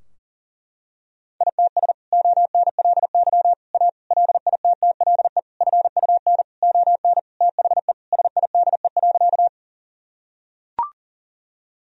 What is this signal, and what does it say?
Morse code, heard as a keyed tone, about 40 wpm, 700 Hz. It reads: ITS ONLY A LITTLE FUN ON THE SIDE.
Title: Sentences from Top 1000 Words - Review Part 4 - 40wpm